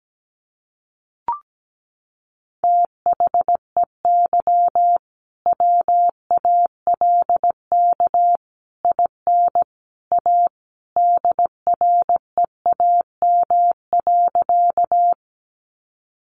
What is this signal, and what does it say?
Morse code, heard as a keyed tone, about 17 wpm, 700 Hz. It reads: THEY WALK IN A DREAM.
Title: Sentences from Top 1500 Words 17wpm